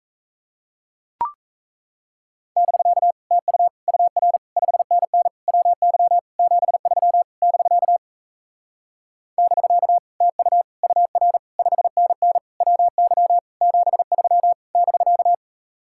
Morse code, heard as a keyed tone, about 38 wpm, 700 Hz. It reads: <BK> TU UR 5NN WY 73 <BK> <BK> TU UR 5NN WY 73 <BK>